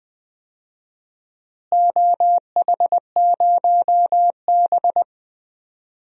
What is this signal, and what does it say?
Morse code, heard as a keyed tone, about 19 wpm, 700 Hz. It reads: OH0B
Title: DXCC-100 Call Signs - 4 Characters - Repeated 3x - 20wpm